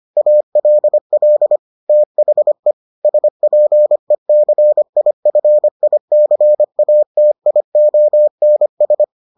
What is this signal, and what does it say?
Morse code, heard as a keyed tone, about 25 wpm, 600 Hz. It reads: ALL THE SPECIFICATIONS